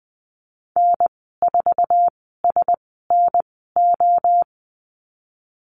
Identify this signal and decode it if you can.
Morse code, heard as a keyed tone, about 20 wpm, spaced as for 14 wpm, 700 Hz. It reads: N4SNO
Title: 055 Lesson 10g - Letter r - Call Signs 20wpm